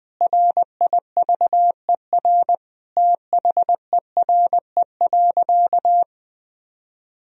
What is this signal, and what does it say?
Morse code, heard as a keyed tone, about 21 wpm, 700 Hz. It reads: RIVER THERE.
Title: Sentences from Top 1000 Words - Review Part 3 - 20wpm